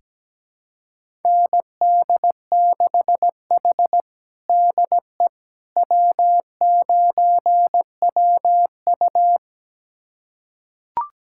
Morse code, heard as a keyed tone, about 17 wpm, 700 Hz. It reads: ND6H DE W9WU
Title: USA Callsign DE Callsign - 4 Characters - 17wpm